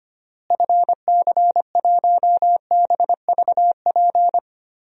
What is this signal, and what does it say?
Morse code, heard as a keyed tone, about 25 wpm, 700 Hz. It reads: FC1BVP
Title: DXCC-100 Call Signs - 6 Characters - Repeated 3x - 25wpm